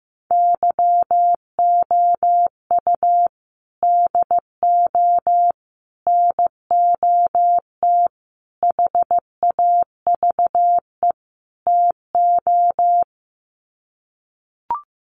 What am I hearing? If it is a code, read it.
Morse code, heard as a keyed tone, about 15 wpm, 700 Hz. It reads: YOU DO NOT HAVE TO
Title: Sets of 5 Words 15wpm